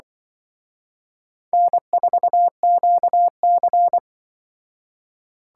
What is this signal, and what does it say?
Morse code, heard as a keyed tone, about 24 wpm, 700 Hz. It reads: N4QC
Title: Speed Racing - USA 1x2 Call Signs 20wpm